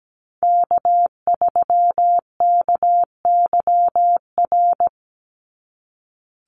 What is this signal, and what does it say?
Morse code, heard as a keyed tone, about 17 wpm, 700 Hz. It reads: K3KYR
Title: NA SOTA Chaser Callsigns 17wpm